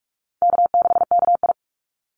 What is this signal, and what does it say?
Morse code, heard as a keyed tone, about 44 wpm, 700 Hz. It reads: K6KI
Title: Easy Call Signs Repeated 3x - 45wpm